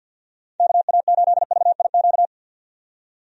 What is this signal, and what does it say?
Morse code, heard as a keyed tone, about 52 wpm, 700 Hz. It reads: KA8VIQ
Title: USA Call Signs - 6 Characters - 50wpm